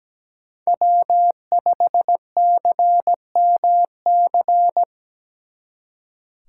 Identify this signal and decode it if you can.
Morse code, heard as a keyed tone, about 20 wpm, 700 Hz. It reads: W5CMC
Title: Speed Racing - USA Call Signs 17wpm